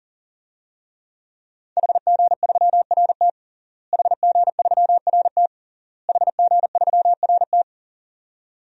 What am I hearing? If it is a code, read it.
Morse code, heard as a keyed tone, about 40 wpm, 700 Hz. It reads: HG3RT HG3RT HG3RT
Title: DXCC-100 Call Signs - 5 Characters - Repeated 3x - 40wpm